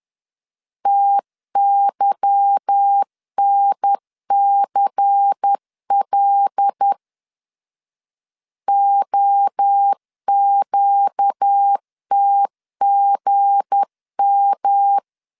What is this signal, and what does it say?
Morse code, heard as a keyed tone, about 10 wpm, 800 Hz. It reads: TYNCL OQTGM
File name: LEZIONI / 8 YJ45 / 8 YJ4 10wpm 2633